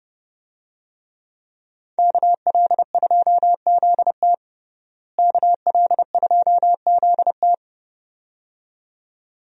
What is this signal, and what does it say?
Morse code, heard as a keyed tone, about 29 wpm, 700 Hz. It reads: KL2ZT KL2ZT
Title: USA Call Signs - 5 Characters - Repeated 2x - 30wpm